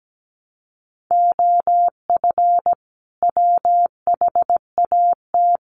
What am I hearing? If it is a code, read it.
Morse code, heard as a keyed tone, about 17 wpm, 700 Hz. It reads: OF WHAT
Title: Sentences from Top 1300 Words 17wpm